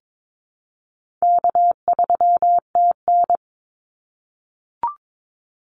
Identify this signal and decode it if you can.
Morse code, heard as a keyed tone, about 22 wpm, 700 Hz. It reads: K3TN
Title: Easy Call Signs - 22wpm